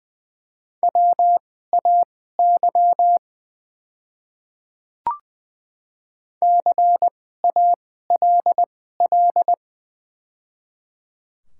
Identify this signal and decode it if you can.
Morse code, heard as a keyed tone, about 20 wpm, spaced as for 15 wpm, 690 Hz. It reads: WAY CALL
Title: 2x Character Spacing - Top 100 Words - 20wpm